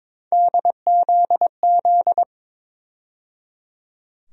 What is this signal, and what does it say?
Morse code, heard as a keyed tone, about 22 wpm, 700 Hz. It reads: DZZ
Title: USA Call Sign Postfixes - 3 Letters 22wpm